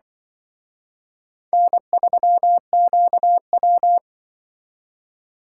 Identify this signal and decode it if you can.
Morse code, heard as a keyed tone, about 25 wpm, 700 Hz. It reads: N3QW
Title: Speed Racing - USA 1x2 Call Signs 20wpm